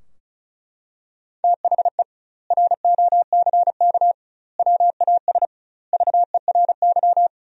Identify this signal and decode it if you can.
Morse code, heard as a keyed tone, about 36 wpm, 700 Hz. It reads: THE ROCK WAS VERY
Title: 2x Word-Spacing - Sentences from Top 400 Words - 35wpm